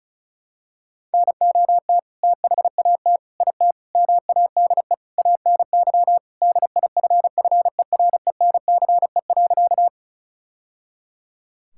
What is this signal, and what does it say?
Morse code, heard as a keyed tone, about 34 wpm, 700 Hz. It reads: NOT THAT IT MADE ANY DIFFERENCE.
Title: Sentences from Top 1000 Words - Review Part 5 - 35wpm